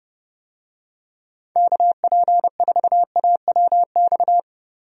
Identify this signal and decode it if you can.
Morse code, heard as a keyed tone, about 31 wpm, 700 Hz. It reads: KP4AWX
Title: DXCC-100 Call Signs - 6 Characters - Repeated 3x - 30wpm